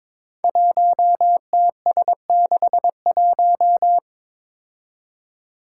Morse code, heard as a keyed tone, about 22 wpm, 700 Hz. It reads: 1TS61
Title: Five Challenging Letter-Number Combinations 22wpm